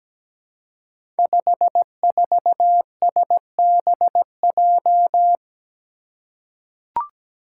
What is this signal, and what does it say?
Morse code, heard as a keyed tone, about 17 wpm, 700 Hz. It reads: 54SBJ
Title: Five Challenging Letter-Number Combinations 17wpm